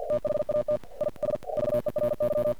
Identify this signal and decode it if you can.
Morse code, heard as a keyed tone, about 50 wpm, 600 Hz. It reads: THAT IS VERY
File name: archived / CRSnet